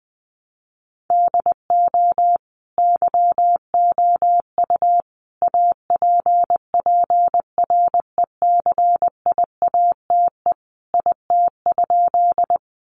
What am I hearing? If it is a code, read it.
Morse code, heard as a keyed tone, about 20 wpm, 700 Hz. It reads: DO YOU APPRECIATE IT?
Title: Sentences from Top 1300 Words 20wpm